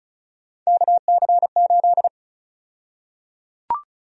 Morse code, heard as a keyed tone, about 34 wpm, 700 Hz. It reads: KC8
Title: USA Call Sign Prefixes - 35wpm